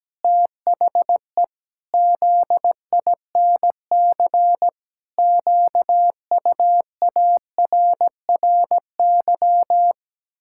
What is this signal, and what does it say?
Morse code, heard as a keyed tone, about 17 wpm, 700 Hz. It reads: THE ZINC QUARRY
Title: Pangram Challenge v2 - 17wpm